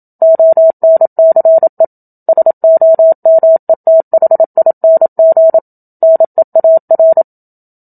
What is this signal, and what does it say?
Morse code, heard as a keyed tone, about 27 wpm, 650 Hz. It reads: ONCE SOMETHING NEAR
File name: Book Two / 01-Chapter 1 - 27 WPM